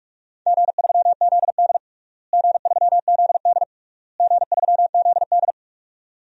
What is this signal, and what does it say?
Morse code, heard as a keyed tone, about 46 wpm, 700 Hz. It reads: G3ZD G3ZD G3ZD
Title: DXCC-100 Call Signs - 4 Characters - Repeated 3x - 45wpm